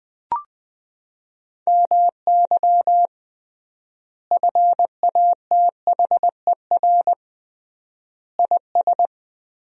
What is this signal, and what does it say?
Morse code, heard as a keyed tone, about 21 wpm, 700 Hz. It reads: MY FATHER IS
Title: 3x Word-Spacing - Sentences from Top 300 Words - 20wpm